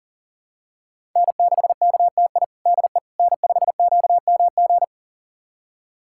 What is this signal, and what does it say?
Morse code, heard as a keyed tone, about 40 wpm, 700 Hz. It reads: N6KTI DE N5QMG
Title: USA Callsign DE Callsign - 5 Characters - 40wpm